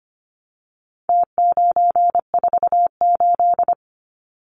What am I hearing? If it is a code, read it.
Morse code, heard as a keyed tone, about 25 wpm, 700 Hz. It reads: T948
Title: Numbers - Contest Style - 4-Digit 25wpm